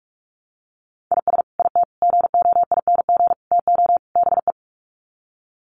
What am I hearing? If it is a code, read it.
Morse code, heard as a keyed tone, about 44 wpm, 700 Hz. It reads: IS IT GOING TO BE
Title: Sets of 5 Words Encore2 45wpm